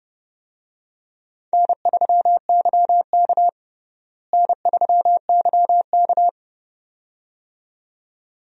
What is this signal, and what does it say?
Morse code, heard as a keyed tone, about 31 wpm, 700 Hz. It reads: N3YK N3YK
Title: USA Call Signs - 4 Characters - Repeated 2x - 30wpm